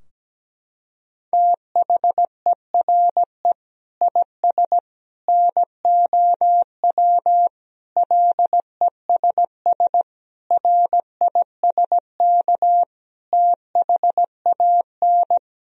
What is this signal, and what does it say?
Morse code, heard as a keyed tone, about 17 wpm, 700 Hz. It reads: THERE IS NOW LESS RISK THAN
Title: Sentences from Top 1000 Words - Review Part 4 - 17wpm